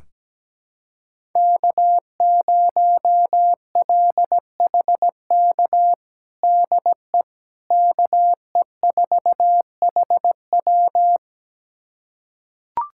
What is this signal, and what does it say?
Morse code, heard as a keyed tone, about 17 wpm, 700 Hz. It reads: K0LHK DE KE4HW
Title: USA Callsign DE Callsign - 5 Characters - 17wpm